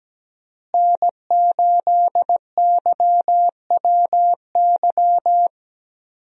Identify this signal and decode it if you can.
Morse code, heard as a keyed tone, about 17 wpm, 700 Hz. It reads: N8YWY
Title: USA Call Signs - 5 Characters - Repeated 3x - 17wpm